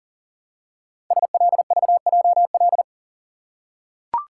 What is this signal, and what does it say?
Morse code, heard as a keyed tone, about 39 wpm, 700 Hz. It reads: SLVJL
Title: Code Groups - 5 Letters - 40wpm